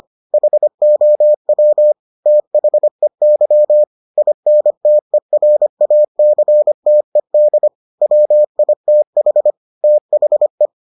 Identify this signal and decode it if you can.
Morse code, heard as a keyed tone, about 25 wpm, 600 Hz. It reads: HOW THEY INTERACTED WITH THE